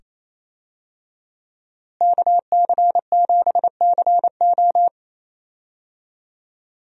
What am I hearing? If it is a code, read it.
Morse code, heard as a keyed tone, about 28 wpm, 700 Hz. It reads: KC7CO
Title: USA Call Signs - 5 Characters - Repeated 2x - 28wpm